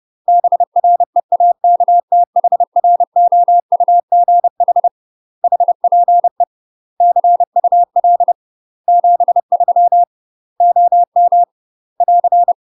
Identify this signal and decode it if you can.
Morse code, heard as a keyed tone, about 30 wpm, 700 Hz. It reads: BREAKTHROUGH HPE CUL 73 OM <AR>
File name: UK MASTER QSOs TRAINING / 30WPM / UK MASTER QSOs 0030